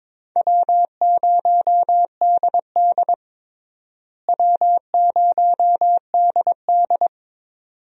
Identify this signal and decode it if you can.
Morse code, heard as a keyed tone, about 22 wpm, 700 Hz. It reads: W0DD W0DD
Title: USA Call Signs - 4 Characters - Repeated 2x - 22wpm